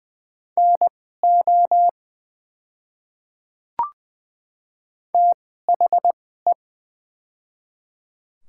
Morse code, heard as a keyed tone, about 20 wpm, spaced as for 14 wpm, 700 Hz. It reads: NO THE
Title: 2x Character Spacing - Top 100 Words - 20wpm